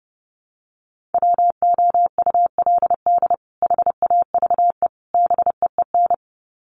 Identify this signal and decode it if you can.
Morse code, heard as a keyed tone, about 30 wpm, 700 Hz. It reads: WOULD HAVE BEEN
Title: Sets of 3 Words 30wpm